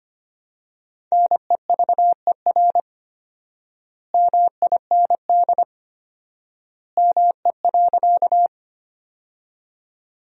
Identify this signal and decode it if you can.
Morse code, heard as a keyed tone, about 25 wpm, 700 Hz. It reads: NEVER MIND ME.
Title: 4x Word-Spacing - Sentences from Top 500 Words - 25wpm